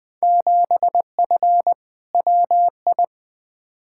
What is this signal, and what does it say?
Morse code, heard as a keyed tone, about 20 wpm, 700 Hz. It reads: 7F WI
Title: Field Day Exchanges Encore - 20wpm